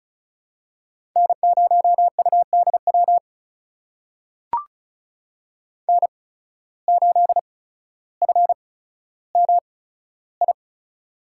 Morse code, heard as a keyed tone, about 36 wpm, spaced as for 15 wpm, 700 Hz. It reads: N0UDW N8FMI
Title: USA Call Signs - 5 Characters - 8x Character Spacing - 35wpm